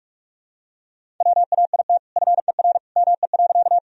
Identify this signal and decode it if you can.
Morse code, heard as a keyed tone, about 43 wpm, 700 Hz. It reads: WAIT FER ME.
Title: Sentences from Top 500 Words and Abbreviations 45wpm